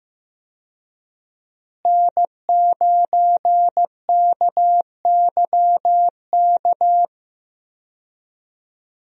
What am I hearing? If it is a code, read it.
Morse code, heard as a keyed tone, about 15 wpm, 700 Hz. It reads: N9KYK
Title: USA Call Signs - 5 Characters - Repeated 2x - 15wpm